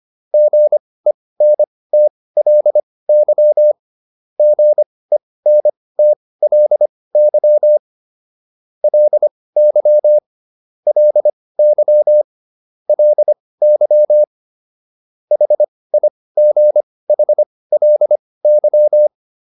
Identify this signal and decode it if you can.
Morse code, heard as a keyed tone, about 25 wpm, spaced as for 18 wpm, 600 Hz. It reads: GENTLY GENTLY LY LY LY HIGHLY
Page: intermediate-practice-files